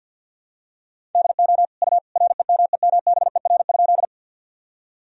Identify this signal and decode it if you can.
Morse code, heard as a keyed tone, about 48 wpm, 700 Hz. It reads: DO U REMEMBER?
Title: Sentences from Top 400 Words and Abbreviations 50wpm